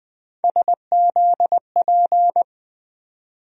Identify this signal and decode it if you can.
Morse code, heard as a keyed tone, about 20 wpm, 700 Hz. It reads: SZP
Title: USA Call Sign Postfixes - 3 Letters 20wpm